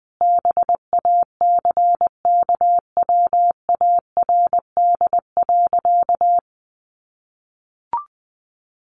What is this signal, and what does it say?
Morse code, heard as a keyed tone, about 19 wpm, 700 Hz. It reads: BACKWARD.
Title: A Sound Of Thunder 20wpm